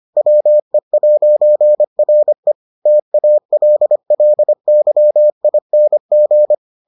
Morse code, heard as a keyed tone, about 25 wpm, 600 Hz. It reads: WE'RE TALLYING